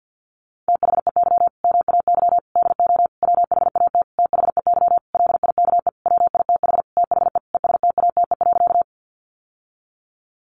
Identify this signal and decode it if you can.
Morse code, heard as a keyed tone, about 48 wpm, 700 Hz. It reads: THEY MAY DO WHAT THEY LIKE WITH THE ESTATE.
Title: Sentences from Top 1200 Words 50wpm